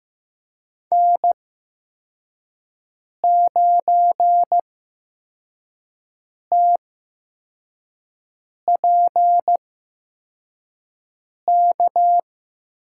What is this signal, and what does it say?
Morse code, heard as a keyed tone, about 15 wpm, spaced as for 4 wpm, 700 Hz. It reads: N9TPK
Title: USA Call Signs - 5 Characters - 8x Character Spacing - 15wpm